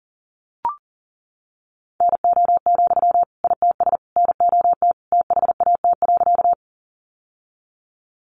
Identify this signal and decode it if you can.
Morse code, heard as a keyed tone, about 40 wpm, 700 Hz. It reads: NO, ITS NOT THAT.
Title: Sentences from Top 600 Words 40wpm